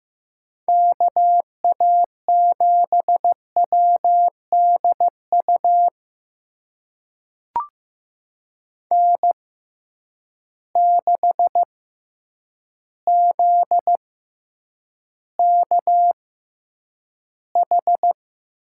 Morse code, heard as a keyed tone, about 15 wpm, spaced as for 8 wpm, 700 Hz. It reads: KA7WDU N6ZKH
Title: USA Call Signs - 6x Character Spacing - 15wpm